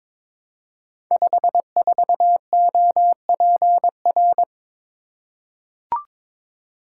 Morse code, heard as a keyed tone, about 22 wpm, 700 Hz. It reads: 54OPR
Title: Code Groups - 5 Letter:Numbers - 22wpm